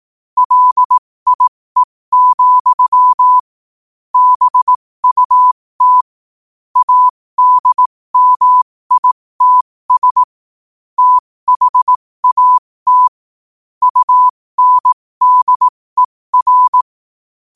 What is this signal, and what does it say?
Morse code, heard as a keyed tone, about 18 wpm, spaced as for 15 wpm, 1000 Hz. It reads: LIE, BUT ADMITS THAT UNDER
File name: qotdfiles / 15wpm